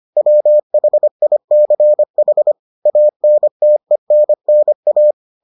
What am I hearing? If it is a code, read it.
Morse code, heard as a keyed tone, about 25 wpm, 600 Hz. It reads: WHICH ANTENNA